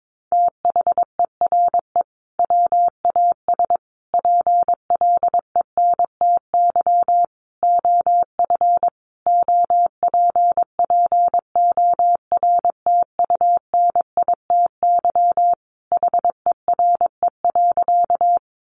Morse code, heard as a keyed tone, about 22 wpm, 700 Hz. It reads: THERE WAS PLENTY OF OPPORTUNITY HERE.
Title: Sentences from Top 1300 Words 22wpm